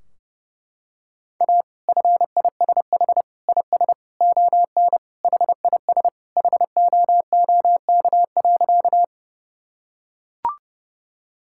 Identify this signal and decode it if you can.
Morse code, heard as a keyed tone, about 30 wpm, 700 Hz. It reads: A FISH IS ON HIS HOOK.
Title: Sentences from Top 1600 Words 30wpm